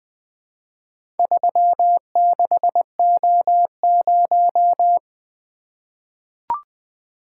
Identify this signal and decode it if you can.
Morse code, heard as a keyed tone, about 20 wpm, 700 Hz. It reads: 36O0
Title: Four Letter-Number Combinations - 20wpm